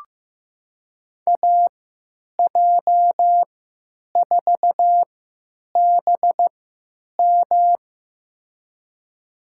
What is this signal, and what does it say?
Morse code, heard as a keyed tone, about 15 wpm, spaced as for 9 wpm, 700 Hz. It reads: AJ4BM
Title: USA Call Signs - 5 Characters - 3x Character Spacing - 15wpm